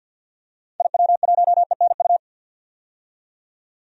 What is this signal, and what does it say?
Morse code, heard as a keyed tone, about 51 wpm, 700 Hz. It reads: IW1ENU
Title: DXCC-100 Call Signs - 6 Characters - 50wpm